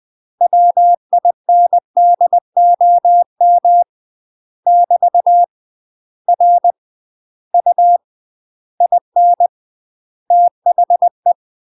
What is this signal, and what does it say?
Morse code, heard as a keyed tone, about 20 wpm, 700 Hz. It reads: WINDOM <BT> R U IN THE
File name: UK MASTER QSOs TRAINING / 20WPM / UK MASTER QSOs 0022